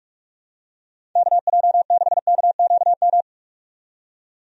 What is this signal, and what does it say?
Morse code, heard as a keyed tone, about 46 wpm, 700 Hz. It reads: KJ6KQM
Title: USA Call Signs - 6 Characters - 45wpm